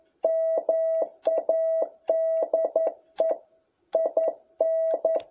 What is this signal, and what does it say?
Morse code, heard as a keyed tone, about 11 wpm, 660 Hz. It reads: MADE IN
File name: jh8cbh / 2021cw